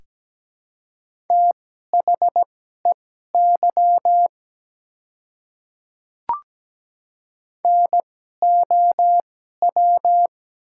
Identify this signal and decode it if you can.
Morse code, heard as a keyed tone, about 17 wpm, spaced as for 12 wpm, 690 Hz. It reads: THEY NOW
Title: 2x Character Spacing - Top 100 Words - 17wpm